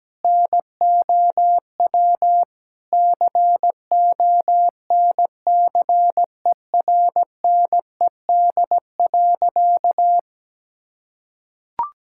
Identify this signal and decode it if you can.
Morse code, heard as a keyed tone, about 17 wpm, 700 Hz. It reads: NOW CONCERNED.
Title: Sentences from Top 1000 Words - Review Part 6 - 17wpm